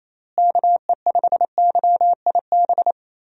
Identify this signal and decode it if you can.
Morse code, heard as a keyed tone, about 28 wpm, 700 Hz. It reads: KE5YIB
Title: USA Call Signs - 6 Characters - Repeated 3x - 28wpm